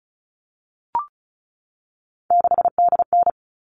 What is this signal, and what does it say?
Morse code, heard as a keyed tone, about 34 wpm, 700 Hz. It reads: 6DN
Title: All Cut Numbers - 3-Digit 35wpm